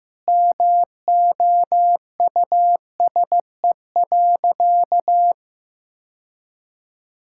Morse code, heard as a keyed tone, about 16 wpm, 700 Hz. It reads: MOUSE.
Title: Sentences from Top 1000 Words - Review Part 5 - 15wpm